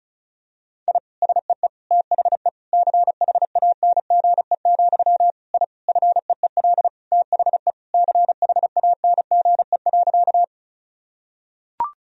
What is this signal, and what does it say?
Morse code, heard as a keyed tone, about 35 wpm, 700 Hz. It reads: I SEE THE CHANGE, I FEEL THE CHANGE.
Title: Sentences from Top 1000 Words - Review Part 1 - 35wpm